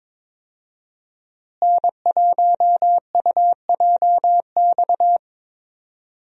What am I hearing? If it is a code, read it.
Morse code, heard as a keyed tone, about 21 wpm, 700 Hz. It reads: N1UJX
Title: USA Call Signs - 5 Characters - Repeated 2x - 22wpm